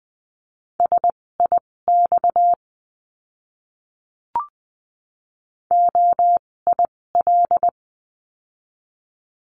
Morse code, heard as a keyed tone, about 20 wpm, spaced as for 16 wpm, 700 Hz. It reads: SIX OIL
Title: 256 Lesson 42a - Words 12-20wpm